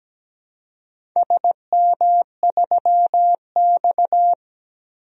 Morse code, heard as a keyed tone, about 17 wpm, 700 Hz. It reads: SM3X
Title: DXCC-100 Call Signs - 4 Characters - 17wpm